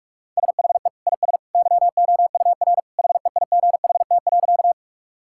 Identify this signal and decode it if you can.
Morse code, heard as a keyed tone, about 47 wpm, 700 Hz. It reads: SHE IS YOUR HEIGHT.
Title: Sentences from Top 1600 Words 45wpm